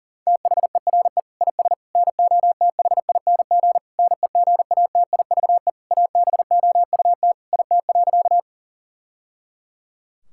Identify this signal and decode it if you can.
Morse code, heard as a keyed tone, about 40 wpm, 700 Hz. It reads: THERE IS NOTHING NEGATIVE ABOUT IT.
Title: Sentences from Top 1000 Words - Review Part 6 - 40wpm